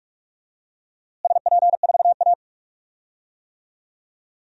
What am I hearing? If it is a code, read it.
Morse code, heard as a keyed tone, about 46 wpm, 700 Hz. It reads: SP4A
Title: DXCC-100 Call Signs - 4 Characters - Repeated 3x - 45wpm